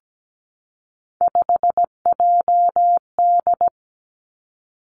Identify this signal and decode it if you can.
Morse code, heard as a keyed tone, about 17 wpm, 700 Hz. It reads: 5JD
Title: Three Challenging Letter-Number Combinations 17wpm